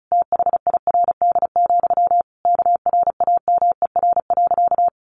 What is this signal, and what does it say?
Morse code, heard as a keyed tone, about 34 wpm, 700 Hz. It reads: THIRD, KRAMER.
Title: A Sound Of Thunder 35wpm